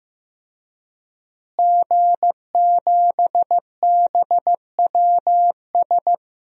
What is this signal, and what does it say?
Morse code, heard as a keyed tone, about 15 wpm, 700 Hz. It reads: G7BWS
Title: DXCC-100 Call Signs - 5 Characters - Repeated 3x - 15wpm